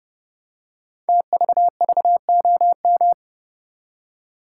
Morse code, heard as a keyed tone, about 30 wpm, 700 Hz. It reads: TVVOM
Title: Code Groups - 5 Letters - 30wpm